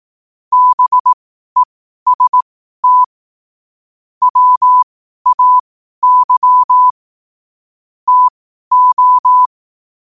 Morse code, heard as a keyed tone, about 18 wpm, spaced as for 12 wpm, 1000 Hz. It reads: BEST WAY TO
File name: qotdfiles / 12wpm